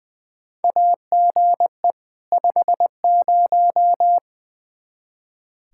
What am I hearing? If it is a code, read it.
Morse code, heard as a keyed tone, about 20 wpm, 700 Hz. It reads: AGE 50
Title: Practice QSO - Exchange 3 - More Info - 20wpm